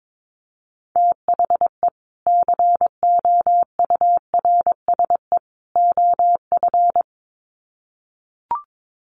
Morse code, heard as a keyed tone, about 22 wpm, 700 Hz. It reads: THE COURSE OF
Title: Sets of 3 Words 22wpm